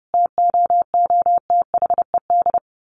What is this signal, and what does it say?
Morse code, heard as a keyed tone, about 30 wpm, 700 Hz. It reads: TOOTHED
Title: A Sound Of Thunder 30wpm